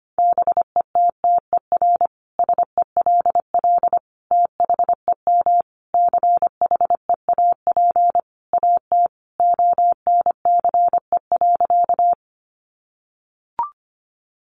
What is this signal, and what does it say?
Morse code, heard as a keyed tone, about 25 wpm, 700 Hz. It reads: BETTER SELL THEM CHEAP AT ONCE.
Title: Sentences from Top 1200 Words 25wpm